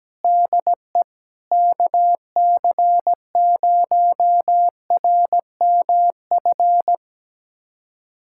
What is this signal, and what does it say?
Morse code, heard as a keyed tone, about 17 wpm, 700 Hz. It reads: DE KC0RMF
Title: USA Callsign DE Callsign 17wpm